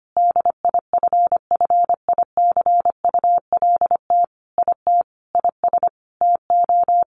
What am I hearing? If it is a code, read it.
Morse code, heard as a keyed tone, about 25 wpm, 700 Hz. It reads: DIFFICULT IT IS TO
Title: Sentences from Top 1100 Words 25wpm